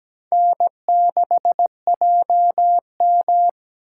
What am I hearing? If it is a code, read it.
Morse code, heard as a keyed tone, about 17 wpm, 700 Hz. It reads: N6JM
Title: USA Call Signs - 4 Characters - 2x Character Spacing - 17wpm